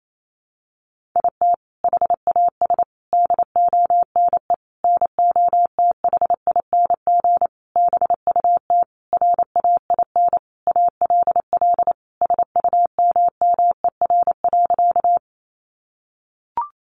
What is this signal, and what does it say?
Morse code, heard as a keyed tone, about 28 wpm, 700 Hz. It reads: IT HAS DONE NOTHING BUT RAIN ALL SUMMER.
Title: Sentences from Top 1100 Words 28wpm